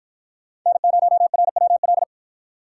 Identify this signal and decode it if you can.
Morse code, heard as a keyed tone, about 53 wpm, 700 Hz. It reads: N0RWL